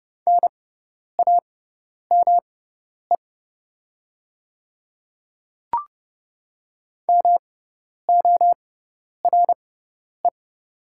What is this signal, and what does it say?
Morse code, heard as a keyed tone, about 32 wpm, spaced as for 10 wpm, 700 Hz. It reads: NAME MORE
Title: 6x Character Spacing - Top 200 Words - 30wpm